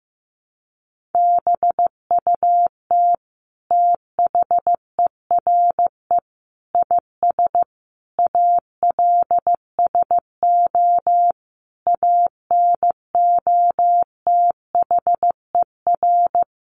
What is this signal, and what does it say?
Morse code, heard as a keyed tone, about 14 wpm, 700 Hz. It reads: BUT THERE IS ALSO ANOTHER
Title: Sentences from Top 1500 Words 15wpm